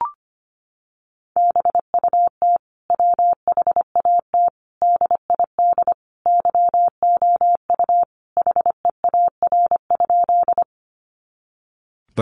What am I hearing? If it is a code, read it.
Morse code, heard as a keyed tone, about 25 wpm, 700 Hz. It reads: BUT WHAT DID YOU HEAR?
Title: Sentences from Top 300 Words 25wpm